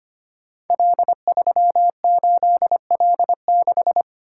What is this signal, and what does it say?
Morse code, heard as a keyed tone, about 25 wpm, 700 Hz. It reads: L38L6
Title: Code Groups - 5 Letter:Numbers - 25wpm